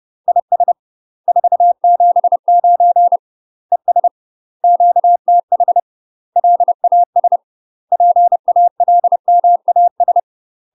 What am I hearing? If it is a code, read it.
Morse code, heard as a keyed tone, about 30 wpm, 700 Hz. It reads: IS 479 ES QTH LAS PALMAS
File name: UK MASTER QSOs TRAINING / 30WPM / UK MASTER QSOs 0023